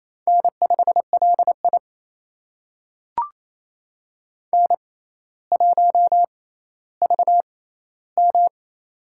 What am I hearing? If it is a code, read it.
Morse code, heard as a keyed tone, about 28 wpm, spaced as for 14 wpm, 700 Hz. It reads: N5LI N1VM
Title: USA Call Signs - 4 Characters - 6x Character Spacing - 28wpm